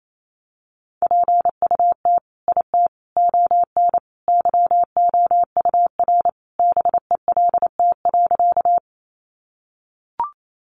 Morse code, heard as a keyed tone, about 28 wpm, 700 Hz. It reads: PUT IT ON YOUR BELT.
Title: Sentences from Top 1400 Words 28wpm